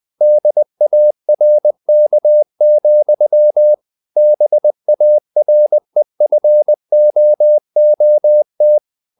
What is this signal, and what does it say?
Morse code, heard as a keyed tone, about 20 wpm, 600 Hz. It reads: DARK, BAREFOOT
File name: the old man and the sea / the old man and the sea part 1 20wpm